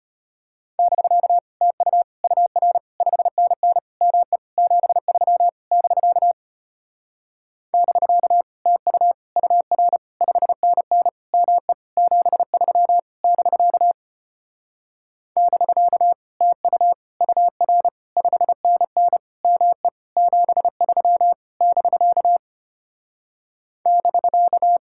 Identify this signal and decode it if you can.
Morse code, heard as a keyed tone, about 33 wpm, 700 Hz. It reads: <BK> TU UR 5NN ME 73 <BK> <BK> TU UR 5NN ME 73 <BK> <BK> TU UR 5NN ME 73 <BK> <BK>